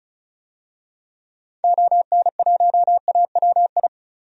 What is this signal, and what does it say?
Morse code, heard as a keyed tone, about 34 wpm, 700 Hz. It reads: ON1AWI
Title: DXCC-100 Call Signs - 6 Characters - 35wpm